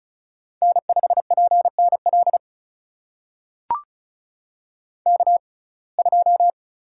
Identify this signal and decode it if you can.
Morse code, heard as a keyed tone, about 36 wpm, spaced as for 25 wpm, 700 Hz. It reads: N5PNL K2
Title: USA Call Signs - 5 Characters - 6x Character Spacing - 35wpm